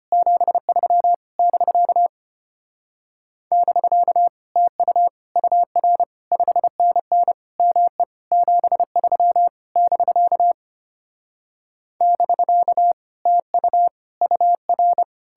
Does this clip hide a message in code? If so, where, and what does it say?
Yes: Morse code from 0.1 to 15.0 s, heard as a keyed tone, about 33 wpm, 700 Hz. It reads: 73 <BK> <BK> TU UR 5NN ME 73 <BK> <BK> TU UR